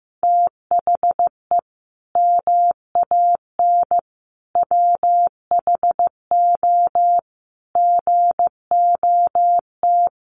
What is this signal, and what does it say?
Morse code, heard as a keyed tone, about 15 wpm, 700 Hz. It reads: THE MAN WHO GOT
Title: Sentences from Top 1400 Words 15wpm